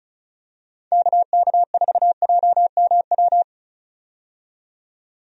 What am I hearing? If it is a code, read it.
Morse code, heard as a keyed tone, about 36 wpm, 700 Hz. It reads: KK4JMW
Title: USA Call Signs - 6 Characters - 35wpm